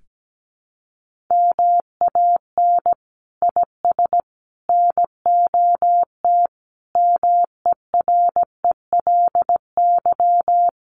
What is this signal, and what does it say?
Morse code, heard as a keyed tone, about 17 wpm, 700 Hz. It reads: MAN IS NOT MERELY
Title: Sentences from Top 1400 Words 17wpm